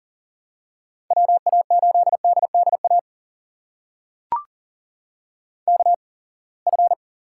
Morse code, heard as a keyed tone, about 40 wpm, spaced as for 27 wpm, 700 Hz. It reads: WA8DDA KF
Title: USA Call Signs - 8x Character Spacing - 40wpm